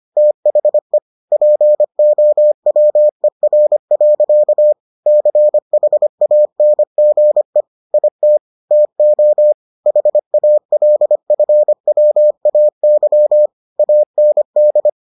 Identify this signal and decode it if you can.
Morse code, heard as a keyed tone, about 25 wpm, 600 Hz. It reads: THE POWER. CHANGE IT TO HALFWAY AND